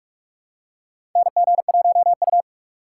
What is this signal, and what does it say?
Morse code, heard as a keyed tone, about 44 wpm, 700 Hz. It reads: NG1U
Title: USA Call Signs - 4 Characters - 3x Character Spacing - 45wpm